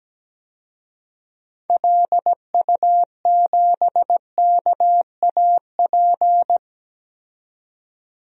Morse code, heard as a keyed tone, about 17 wpm, 700 Hz. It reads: LU7KAP
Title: DXCC-100 Call Signs - 6 Characters - 17wpm